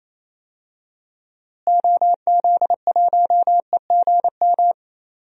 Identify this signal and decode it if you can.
Morse code, heard as a keyed tone, about 28 wpm, 700 Hz. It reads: OZ1EGM
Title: DXCC-100 Call Signs - 6 Characters - Repeated 3x - 28wpm